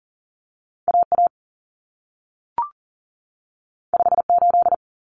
Morse code, heard as a keyed tone, about 40 wpm, 700 Hz. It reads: AA 58
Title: Common Cut Numbers - 2-Digit 40wpm